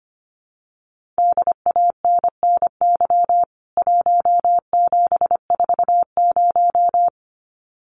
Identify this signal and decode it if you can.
Morse code, heard as a keyed tone, about 25 wpm, 700 Hz. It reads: DANNY 1740
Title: CWT Exchange 25wpm